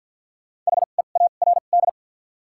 Morse code, heard as a keyed tone, about 53 wpm, spaced as for 37 wpm, 700 Hz. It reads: HEARD
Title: 2x Character Spacing - Top 500 Words - 50wpm